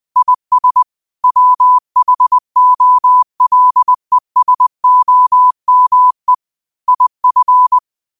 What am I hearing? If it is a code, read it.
Morse code, heard as a keyed tone, about 20 wpm, 1000 Hz. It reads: IS WHOLESOME IF
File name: qotdfiles / 20wpm